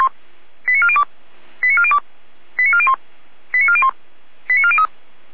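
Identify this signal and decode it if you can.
Five-tone selective call (ZVEI), heard as quick runs of five tones, tones of about 70 ms. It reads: 89402 89402 89401 89401 89403
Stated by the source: ZVEI five-tone selective call received by SDR.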